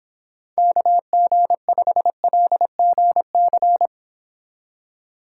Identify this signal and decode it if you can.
Morse code, heard as a keyed tone, about 30 wpm, 700 Hz. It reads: KG5LGC